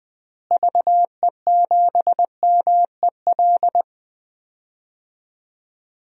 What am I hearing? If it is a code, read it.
Morse code, heard as a keyed tone, about 20 wpm, 700 Hz. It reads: VE7MEL
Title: DXCC-100 Call Signs - 6 Characters - Repeated 3x - 20wpm